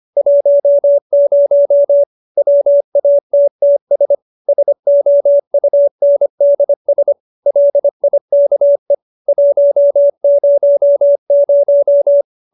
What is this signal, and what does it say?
Morse code, heard as a keyed tone, about 25 wpm, 570 Hz. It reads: 10 WATTS SOUNDS LIKE 100